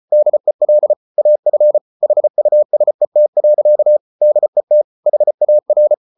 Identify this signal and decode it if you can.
Morse code, heard as a keyed tone, about 34 wpm, 600 Hz. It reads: DEL AF HUSET. DET HAR
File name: Ferie QRQ 34wpm